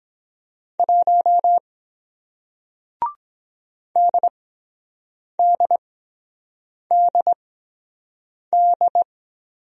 Morse code, heard as a keyed tone, about 23 wpm, spaced as for 7 wpm, 700 Hz. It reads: 1 DDDD